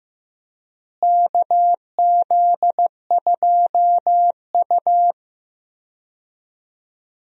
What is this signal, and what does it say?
Morse code, heard as a keyed tone, about 15 wpm, 700 Hz. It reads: KZ2U
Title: USA Call Signs - 4 Characters - Repeated 2x - 15wpm